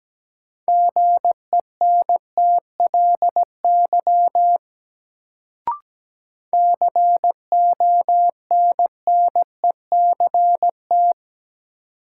Morse code, heard as a keyed tone, about 17 wpm, 700 Hz. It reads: GENTLY CONNECT
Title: Sentences from Top 1500 Words - New Word Drill - 17wpm